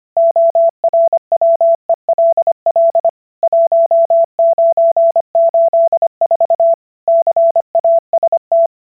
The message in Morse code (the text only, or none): ORWELL 1984 CAST